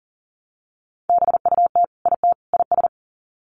Morse code, heard as a keyed tone, about 40 wpm, 700 Hz. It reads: BUT IT IS
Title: Sets of 3 Words 40wpm